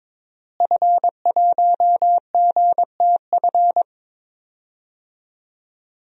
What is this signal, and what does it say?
Morse code, heard as a keyed tone, about 21 wpm, 700 Hz. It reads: F1GTF
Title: DXCC-100 Call Signs - 5 Characters - Repeated 2x - 22wpm